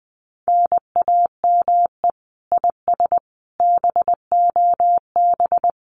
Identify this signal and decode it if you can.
Morse code, heard as a keyed tone, about 20 wpm, 700 Hz. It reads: NAME IS BOB
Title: Common QSO Phrases 20wpm